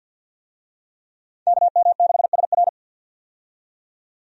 Morse code, heard as a keyed tone, about 52 wpm, 700 Hz. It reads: KM6SR
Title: USA Call Signs - 5 Characters - 50wpm